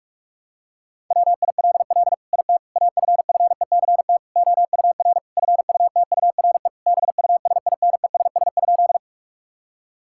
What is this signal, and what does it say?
Morse code, heard as a keyed tone, about 42 wpm, 700 Hz. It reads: WILL IT AFFECT OUR FUTURE BUSINESS?
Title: Sentences from Top 1000 Words - Review Part 6 - 45wpm